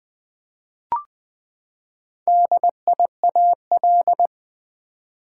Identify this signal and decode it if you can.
Morse code, heard as a keyed tone, about 21 wpm, 700 Hz. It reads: DIAL
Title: Build A Word - 4 Letters At A Time - 8-Letter Words - 20wpm